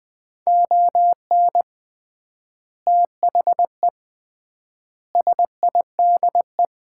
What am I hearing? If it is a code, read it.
Morse code, heard as a keyed tone, about 20 wpm, 700 Hz. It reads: ON THE SIDE
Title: 3x Word-Spacing - Sets of 4 Words - 20wpm